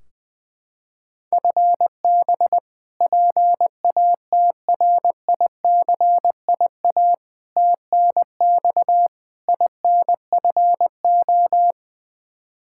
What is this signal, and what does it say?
Morse code, heard as a keyed tone, about 21 wpm, 700 Hz. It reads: FB PATRICIA TNX INFO
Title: Practice QSO - Exchange 3 - More Info - 20wpm